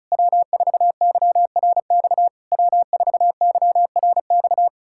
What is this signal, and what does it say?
Morse code, heard as a keyed tone, about 40 wpm, 700 Hz. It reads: W4YRX W4YRX